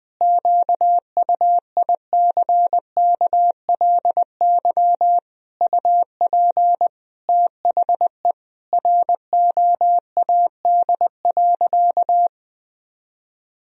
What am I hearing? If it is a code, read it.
Morse code, heard as a keyed tone, about 21 wpm, 700 Hz. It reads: QUICKLY UP THE ROAD.
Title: Sentences from Top 1000 Words - Review Part 2 - 20wpm